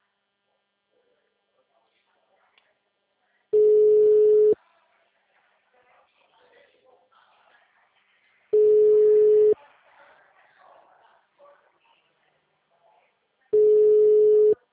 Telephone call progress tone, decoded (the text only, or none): ringback tone